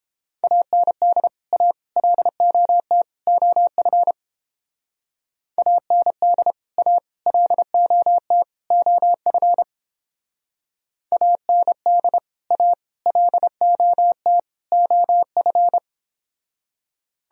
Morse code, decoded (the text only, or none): AND A LOT OF AND A LOT OF AND A LOT OF